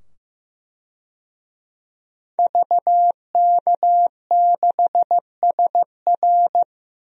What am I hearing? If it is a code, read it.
Morse code, heard as a keyed tone, about 15 wpm, 700 Hz. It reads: VK6SR
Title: DXCC-100 Call Signs - 5 Characters - 15wpm